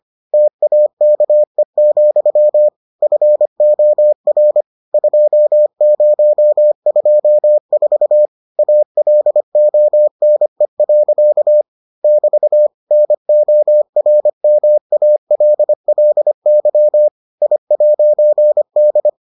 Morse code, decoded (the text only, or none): TAKE, FOR 2024 ALONE. <BT> NORMALLY I'D